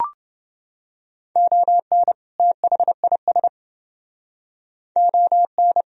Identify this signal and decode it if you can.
Morse code, heard as a keyed tone, about 29 wpm, 700 Hz. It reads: ON THIS ON